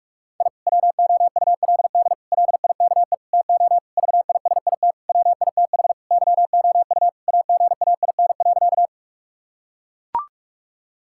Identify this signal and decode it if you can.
Morse code, heard as a keyed tone, about 42 wpm, 700 Hz. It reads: I WOULD LIKE TO VISIT WITH YOU AGAIN.
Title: Sentences from Top 1000 Words - Review Part 6 - 45wpm